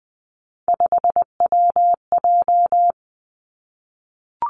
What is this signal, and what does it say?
Morse code, heard as a keyed tone, about 20 wpm, 700 Hz. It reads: 5WJ
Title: Three Challenging Letter-Number Combinations 20wpm